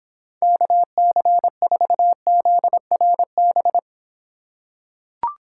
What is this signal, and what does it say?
Morse code, heard as a keyed tone, about 26 wpm, 700 Hz. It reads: KC4ZRB